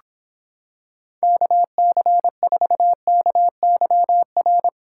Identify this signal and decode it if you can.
Morse code, heard as a keyed tone, about 26 wpm, 700 Hz. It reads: KC4KYR